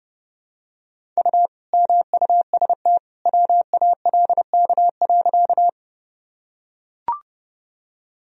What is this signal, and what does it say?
Morse code, heard as a keyed tone, about 31 wpm, 700 Hz. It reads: U MUST WALK.
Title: Sentences from Top 300 Words and Abbreviations 30wpm